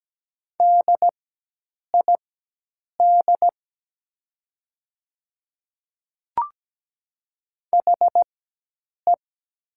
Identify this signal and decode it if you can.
Morse code, heard as a keyed tone, about 17 wpm, spaced as for 8 wpm, 700 Hz. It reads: DID HE
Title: 4x Character Spacing - Top 100 Words - 17wpm